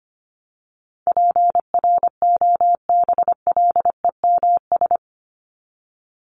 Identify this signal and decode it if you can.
Morse code, heard as a keyed tone, about 25 wpm, 700 Hz. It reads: PROBLEMS
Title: ES:S Suffix - Top 100 Words with Base Words 25wpm